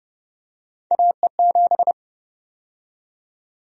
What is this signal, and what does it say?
Morse code, heard as a keyed tone, about 30 wpm, 700 Hz. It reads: AE7
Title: USA Call Sign Prefixes - 30wpm